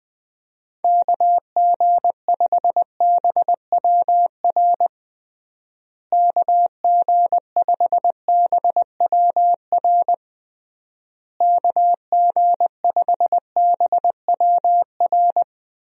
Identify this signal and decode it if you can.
Morse code, heard as a keyed tone, about 20 wpm, 700 Hz. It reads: KG5BWR KG5BWR KG5BWR
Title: USA Call Signs - 6 Characters - Repeated 3x - 20wpm